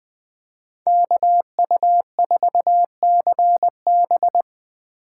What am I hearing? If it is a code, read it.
Morse code, heard as a keyed tone, about 20 wpm, 700 Hz. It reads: KU4CB
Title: USA Call Signs - 5 Characters - 4x Character Spacing - 20wpm